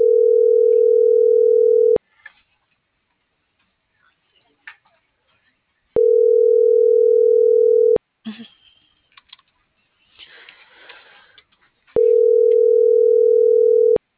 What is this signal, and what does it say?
Ringback tone (North America), 440 + 480 Hz.
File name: CENTRAL MORENA / out jacque.aguilera.25 - .7897